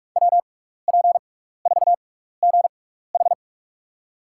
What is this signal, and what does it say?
Morse code, heard as a keyed tone, about 44 wpm, spaced as for 15 wpm, 690 Hz. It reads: WP4GH
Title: USA Call Signs - 5 Characters - 6x Character Spacing - 45wpm